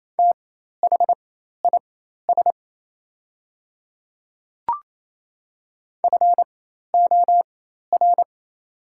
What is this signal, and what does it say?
Morse code, heard as a keyed tone, about 28 wpm, spaced as for 13 wpm, 700 Hz. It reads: THIS FOR
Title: 4x Character Spacing - Top 200 Words - 28wpm